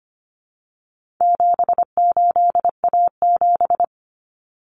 Morse code, heard as a keyed tone, about 25 wpm, 700 Hz. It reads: 78A7
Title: Common Cut Numbers - 4-Digit 25wpm